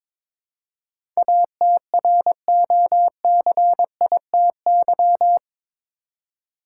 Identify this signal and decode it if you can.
Morse code, heard as a keyed tone, about 22 wpm, 700 Hz. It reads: ATROCITY
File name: practice / 22wpm / Eight Letter Words - 22wpm